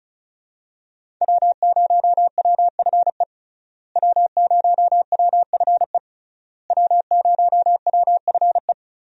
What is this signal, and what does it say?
Morse code, heard as a keyed tone, about 36 wpm, 700 Hz. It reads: W0WFE W0WFE W0WFE
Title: USA Call Signs - 5 Characters - Repeated 3x - 35wpm